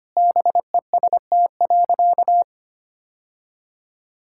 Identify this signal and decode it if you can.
Morse code, heard as a keyed tone, about 25 wpm, 700 Hz. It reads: BEST.
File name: practice / 25wpm / Sentences with 3-Word Phrases - 25wpm